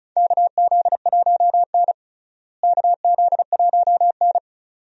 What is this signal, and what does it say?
Morse code, heard as a keyed tone, about 34 wpm, 700 Hz. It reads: KZ1N KZ1N
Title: USA Call Signs - 4 Characters - Repeated 2x - 35wpm